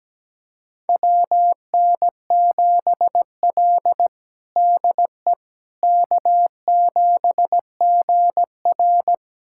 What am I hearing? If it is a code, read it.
Morse code, heard as a keyed tone, about 17 wpm, 700 Hz. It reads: WN7L DE K7GR
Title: USA Callsign DE Callsign - 4 Characters - 17wpm